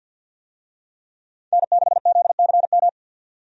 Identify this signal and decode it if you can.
Morse code, heard as a keyed tone, about 52 wpm, 700 Hz. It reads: N6ZCM
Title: USA Call Signs - 5 Characters - 50wpm